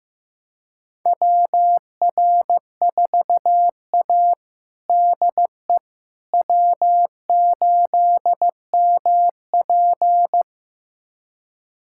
Morse code, heard as a keyed tone, about 15 wpm, 700 Hz. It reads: WR4A DE W8MP
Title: USA Callsign DE Callsign - 4 Characters - 15wpm